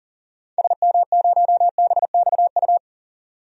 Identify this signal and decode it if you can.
Morse code, heard as a keyed tone, about 40 wpm, 700 Hz. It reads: SM0BXU
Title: DXCC-100 Call Signs - 6 Characters - 40wpm